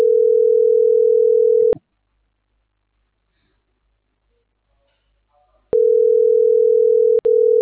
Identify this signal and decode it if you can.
Ringback tone (North America), 440 + 480 Hz.